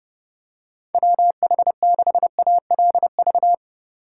Morse code, heard as a keyed tone, about 31 wpm, 700 Hz. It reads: WH6ALV